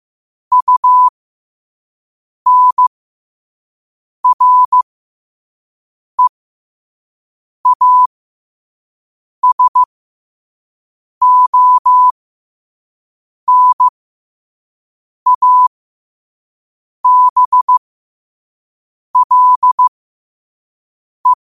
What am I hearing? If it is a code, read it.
Morse code, heard as a keyed tone, about 15 wpm, spaced as for 5 wpm, 1000 Hz. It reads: UNREASONABLE
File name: qotdfiles / 5wpm